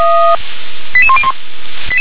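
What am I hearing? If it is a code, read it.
Five-tone selective call (ZVEI), heard as quick runs of five tones, tones of about 70 ms. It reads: 88101
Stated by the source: ZVEI five-tone selective call 88665, received by SDR.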